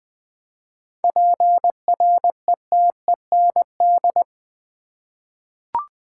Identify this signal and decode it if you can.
Morse code, heard as a keyed tone, about 21 wpm, 700 Hz. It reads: PRETEND
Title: Seven Letter Words - 20wpm